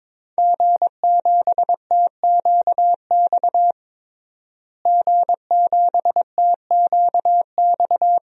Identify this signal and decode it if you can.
Morse code, heard as a keyed tone, about 22 wpm, 700 Hz. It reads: G7TQX G7TQX
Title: DXCC-100 Call Signs - 5 Characters - Repeated 3x - 22wpm